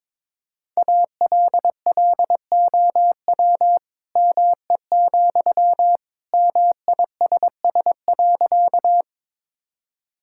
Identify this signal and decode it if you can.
Morse code, heard as a keyed tone, about 22 wpm, 700 Hz. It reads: ALLOW ME, MISS.
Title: Sentences from Top 1000 Words - Review Part 5 - 22wpm